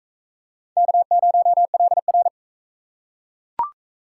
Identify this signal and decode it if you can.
Morse code, heard as a keyed tone, about 39 wpm, 700 Hz. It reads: K0LR